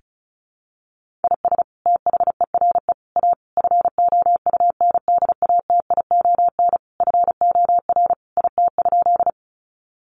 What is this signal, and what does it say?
Morse code, heard as a keyed tone, about 34 wpm, 700 Hz. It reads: IS THERE A FOUNDATION FOR IT?
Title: Sentences from Top 1200 Words 35wpm